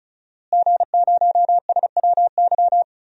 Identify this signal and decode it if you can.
Morse code, heard as a keyed tone, about 36 wpm, 700 Hz. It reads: G0SWY
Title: DXCC-100 Call Signs - 5 Characters - Repeated 2x - 35wpm